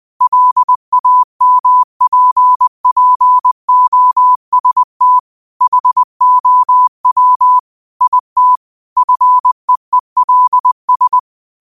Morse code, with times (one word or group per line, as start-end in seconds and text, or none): 0.2-5.2 LAMPPOST
5.6-7.6 HOW
8.0-8.6 IT
9.0-11.2 FEELS